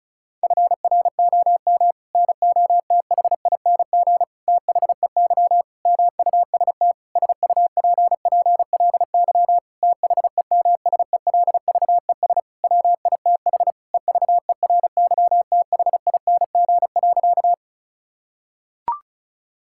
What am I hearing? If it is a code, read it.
Morse code, heard as a keyed tone, about 34 wpm, 700 Hz. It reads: FROM NOTHING THEY MUST SUPPLY THEMSELVES WITH EVERYTHING.
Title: Sentences from Top 1000 Words - Review Part 5 - 35wpm